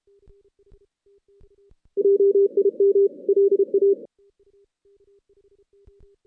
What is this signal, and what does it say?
Morse code, heard as a keyed tone, about 31 wpm, 400 Hz. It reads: JIM LA
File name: 2016 JAN CW NAQP / 2016 Jan naqp k5ot - 0-11